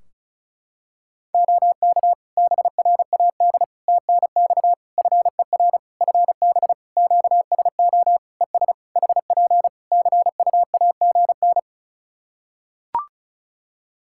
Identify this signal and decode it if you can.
Morse code, heard as a keyed tone, about 34 wpm, 700 Hz. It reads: OK BRAD TNX FER FB QSO ES HP CUAGN
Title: K1USN SST QSO Practice 35wpm